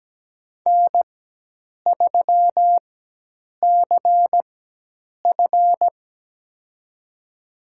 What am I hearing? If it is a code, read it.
Morse code, heard as a keyed tone, about 17 wpm, spaced as for 8 wpm, 700 Hz. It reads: N3CF
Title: USA Call Signs - 4 Characters - 4x Character Spacing - 17wpm